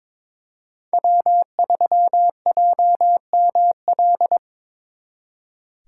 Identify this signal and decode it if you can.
Morse code, heard as a keyed tone, about 26 wpm, 700 Hz. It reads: W3JML
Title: Speed Racing - USA 1x3 Call Signs 22wpm